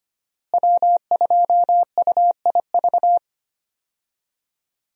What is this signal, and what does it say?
Morse code, heard as a keyed tone, about 25 wpm, 700 Hz. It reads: W2UIV
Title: USA Call Signs - 5 Characters - Repeated 3x - 25wpm